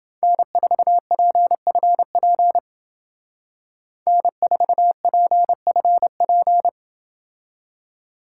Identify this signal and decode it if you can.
Morse code, heard as a keyed tone, about 28 wpm, 700 Hz. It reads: N4PFP N4PFP